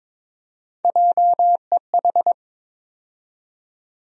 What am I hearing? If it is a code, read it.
Morse code, heard as a keyed tone, about 22 wpm, 700 Hz. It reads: JEH
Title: USA Call Sign Postfixes - 3 Letters 22wpm